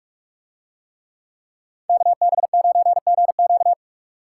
Morse code, heard as a keyed tone, about 44 wpm, 700 Hz. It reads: KB9GQ